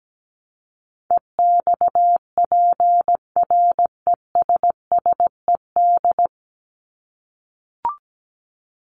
Morse code, heard as a keyed tone, about 17 wpm, 700 Hz. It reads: EXPRESSED
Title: ED Suffix - Top 100 Words with Base Words 17wpm